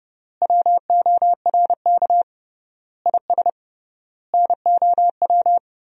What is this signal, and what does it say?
Morse code, heard as a keyed tone, about 32 wpm, 700 Hz. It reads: WORK IS NOW
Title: 3x Word-Spacing - Sentences from Top 400 Words - 30wpm